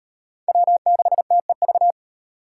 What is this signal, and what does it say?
Morse code, heard as a keyed tone, about 34 wpm, 700 Hz. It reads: W6TEV